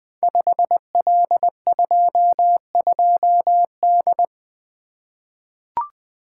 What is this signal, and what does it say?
Morse code, heard as a keyed tone, about 20 wpm, 700 Hz. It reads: 5L22D
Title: Code Groups - 5 Letter:Numbers - 20wpm